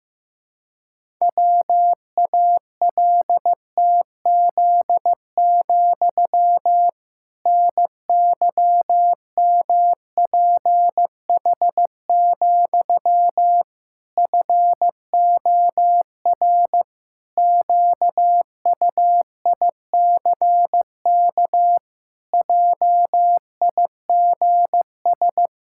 Morse code, heard as a keyed tone, about 16 wpm, 700 Hz. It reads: WALTZ, NYMPH, FOR QUICK JIGS
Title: Pangram Challenge v2 - 15wpm